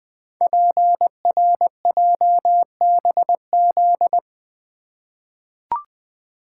Morse code, heard as a keyed tone, about 20 wpm, 700 Hz. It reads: PRJBZ
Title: Code Groups - 5 Letters - 20wpm